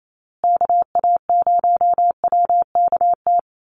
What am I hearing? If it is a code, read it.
Morse code, heard as a keyed tone, about 28 wpm, 700 Hz. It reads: KA0WKT
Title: USA Call Signs Repeated 2x 28wpm